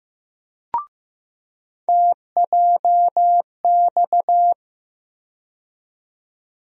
Morse code, heard as a keyed tone, about 15 wpm, 700 Hz. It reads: TJX
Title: USA Call Sign Postfixes - 3 Letters 15wpm